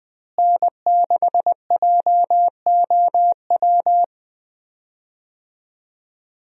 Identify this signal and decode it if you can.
Morse code, heard as a keyed tone, about 20 wpm, 700 Hz. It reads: N6JOW
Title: USA Call Signs - 5 Characters - Repeated 2x - 20wpm